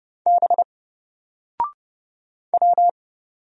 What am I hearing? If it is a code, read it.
Morse code, heard as a keyed tone, about 30 wpm, 700 Hz. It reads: B W
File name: Single-Letter-Number-30wpm